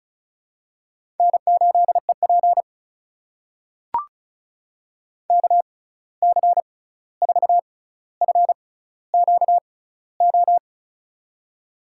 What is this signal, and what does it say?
Morse code, heard as a keyed tone, about 36 wpm, spaced as for 16 wpm, 700 Hz. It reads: N8EP KC4FQO
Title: USA Call Signs - 6x Character Spacing - 35wpm